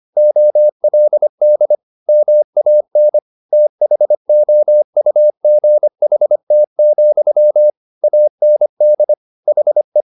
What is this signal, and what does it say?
Morse code, heard as a keyed tone, about 25 wpm, 600 Hz. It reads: OLD MAN THOUGHT, AND HE
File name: the old man and the sea part 1 25wpm